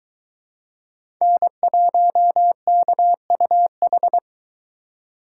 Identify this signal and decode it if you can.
Morse code, heard as a keyed tone, about 23 wpm, 700 Hz. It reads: N1KUH